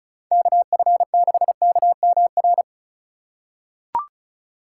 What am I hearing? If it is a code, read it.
Morse code, heard as a keyed tone, about 34 wpm, 700 Hz. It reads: KF6KMR
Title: USA Call Signs - 2x Character Spacing - 35wpm